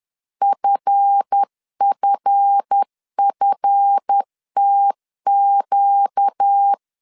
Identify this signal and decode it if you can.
Morse code, heard as a keyed tone, about 10 wpm, 800 Hz. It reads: FFFTQ
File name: LEZIONI / 7 GF9Z / 7 GF 10wpm 2514